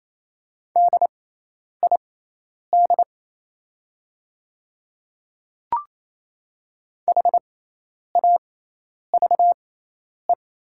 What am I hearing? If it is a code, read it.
Morse code, heard as a keyed tone, about 27 wpm, spaced as for 10 wpm, 700 Hz. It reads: DID HAVE
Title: 6x Character Spacing - Top 100 Words - 28wpm